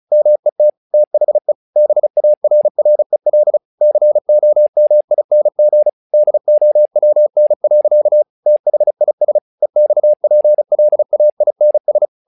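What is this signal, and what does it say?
Morse code, heard as a keyed tone, about 36 wpm, 600 Hz. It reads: MET THE BARREL COMING DOWN. THIS EXPLAINS
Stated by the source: Morse practice text "Tower Trouble" sent at 35 WPM, from the SKCC - Straight Key Century Club learning center audio files.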